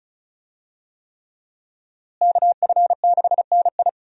Morse code, heard as a keyed tone, about 36 wpm, 700 Hz. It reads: KF6NI
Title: USA Call Signs - 5 Characters - Repeated 3x - 35wpm